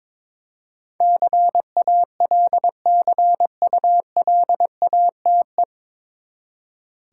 Morse code, heard as a keyed tone, about 22 wpm, 700 Hz. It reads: CALCULATE
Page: Morse Code Ninja practice 22wpm